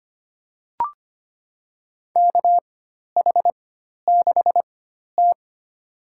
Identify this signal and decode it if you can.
Morse code, heard as a keyed tone, about 25 wpm, spaced as for 12 wpm, 700 Hz. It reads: KH6T
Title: USA Call Signs - 4 Characters - 4x Character Spacing - 25wpm